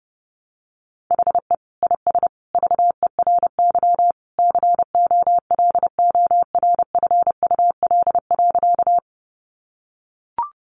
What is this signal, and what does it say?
Morse code, heard as a keyed tone, about 30 wpm, 700 Hz. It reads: HE IS VERY COLORFUL.
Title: Easy Phrases 30wpm